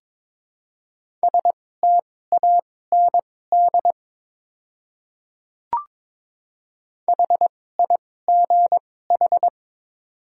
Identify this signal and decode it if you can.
Morse code, heard as a keyed tone, about 22 wpm, spaced as for 16 wpm, 700 Hz. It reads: STAND HIGH
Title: 2x Character Spacing - Top 300 Words - 22wpm